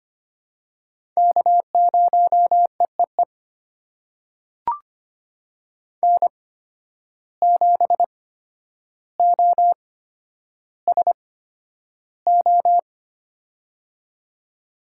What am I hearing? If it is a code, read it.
Morse code, heard as a keyed tone, about 25 wpm, spaced as for 11 wpm, 700 Hz. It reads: K0EEE N7OSO
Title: USA Call Signs - 5 Characters - 8x Character Spacing - 25wpm